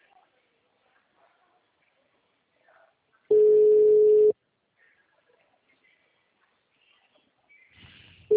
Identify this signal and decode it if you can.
Ringback tone (Europe, 425 Hz).